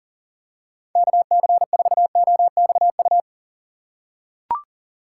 Morse code, heard as a keyed tone, about 39 wpm, 700 Hz. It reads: KC4OXU
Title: USA Call Signs - 6 Characters - 4x Character Spacing - 40wpm